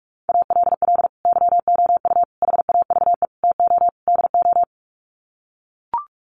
Morse code, heard as a keyed tone, about 44 wpm, 700 Hz. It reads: ALL YOU HAVE TO DO
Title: Sets of 5 Words 45wpm